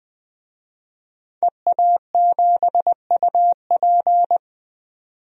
Morse code, heard as a keyed tone, about 20 wpm, 700 Hz. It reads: EA7UP
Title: DXCC-100 Call Signs - 5 Characters - 20wpm